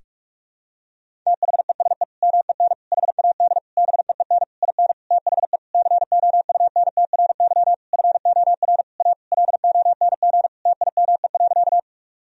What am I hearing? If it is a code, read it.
Morse code, heard as a keyed tone, about 41 wpm, 700 Hz. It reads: THESE MEN HAD BEEN IN THE COUNTRY FOR A LONG TIME.
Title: Sentences from Top 1000 Words - Review Part 1 - 45wpm